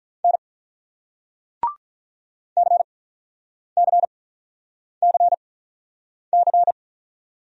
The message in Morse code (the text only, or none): N CCCC